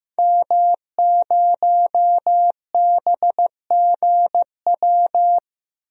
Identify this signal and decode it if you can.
Morse code, heard as a keyed tone, about 15 wpm, 700 Hz. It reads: M0BGW
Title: DXCC-100 Call Signs - 5 Characters - Repeated 2x - 15wpm